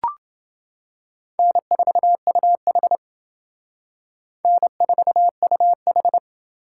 Morse code, heard as a keyed tone, about 29 wpm, 700 Hz. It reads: N4UH N4UH